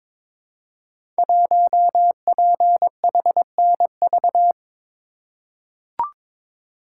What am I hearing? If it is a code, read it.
Morse code, heard as a keyed tone, about 22 wpm, 700 Hz. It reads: 1PHNV
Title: Code Groups - 5 Letter:Numbers - 22wpm